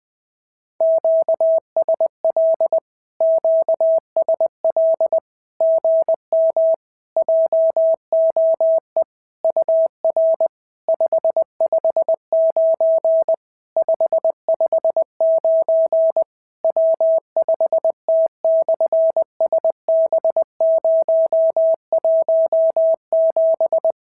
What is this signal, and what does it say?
Morse code, heard as a keyed tone, about 20 wpm, 650 Hz. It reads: QSL QSL GM JOE UR 559 559 W5T/SB017
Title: SOTA QSO Practice 20wpm